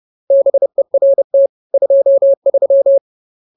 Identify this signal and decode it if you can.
Morse code, heard as a keyed tone, about 30 wpm, 550 Hz. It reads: BERT 23